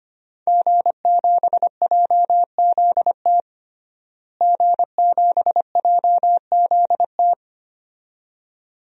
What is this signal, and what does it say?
Morse code, heard as a keyed tone, about 25 wpm, 700 Hz. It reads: G7JZT G7JZT
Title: DXCC-100 Call Signs - 5 Characters - Repeated 2x - 25wpm